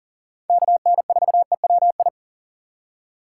Morse code, heard as a keyed tone, about 38 wpm, 690 Hz. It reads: KN4EWI
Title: K1USN SST QSO Practice - 40wpm